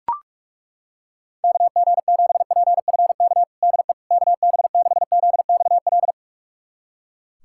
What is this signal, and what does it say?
Morse code, heard as a keyed tone, about 41 wpm, 690 Hz. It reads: KG7PFK DE KB6ZXL
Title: USA Callsign DE Callsign - 6 Characters - 45wpm